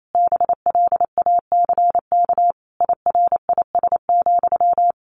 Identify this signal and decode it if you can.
Morse code, heard as a keyed tone, about 28 wpm, 700 Hz. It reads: BLACK IRIS,
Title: A Sound Of Thunder 28wpm